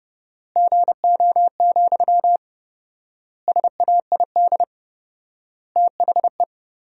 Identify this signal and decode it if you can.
Morse code, heard as a keyed tone, about 30 wpm, 700 Hz. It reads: GO, SAID THE
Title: 4x Word-Spacing - Sentences from Top 400 Words - 30wpm